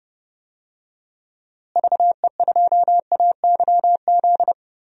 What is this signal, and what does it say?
Morse code, heard as a keyed tone, about 30 wpm, 700 Hz. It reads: VE2AYZ
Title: DXCC-100 Call Signs - 6 Characters - Repeated 3x - 30wpm